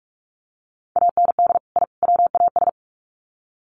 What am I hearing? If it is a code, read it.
Morse code, heard as a keyed tone, about 44 wpm, 700 Hz. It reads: AND I WAS
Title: Sets of 3 Words 45wpm